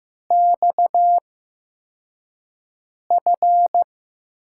Morse code, heard as a keyed tone, about 15 wpm, spaced as for 4 wpm, 700 Hz. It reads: XF
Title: USA Call Signs - 6 Characters - 8x Character Spacing - 15wpm